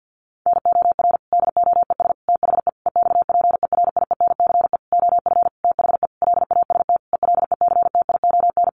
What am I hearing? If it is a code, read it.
Morse code, heard as a keyed tone, about 50 wpm, 700 Hz. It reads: NOR DOES THE EXPERIENCE OF THE LAST ELECTION
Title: Sentences from Top 1500 Words 50wpm